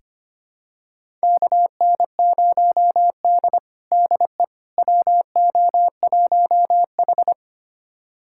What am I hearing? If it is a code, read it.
Morse code, heard as a keyed tone, about 25 wpm, 700 Hz. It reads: KN0D DE WO1H
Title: USA Callsign DE Callsign - 4 Characters - 25wpm